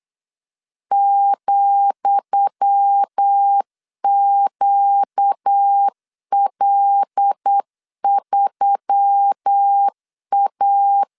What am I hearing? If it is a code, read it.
Morse code, heard as a keyed tone, about 8 wpm, 800 Hz. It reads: ,QL3A
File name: LEZIONI / 10 SIGNS / 10D COMA 8wpm 3749